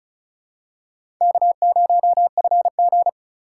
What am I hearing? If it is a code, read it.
Morse code, heard as a keyed tone, about 36 wpm, 700 Hz. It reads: K0FG
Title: USA Call Signs - 4 Characters - 2x Character Spacing - 35wpm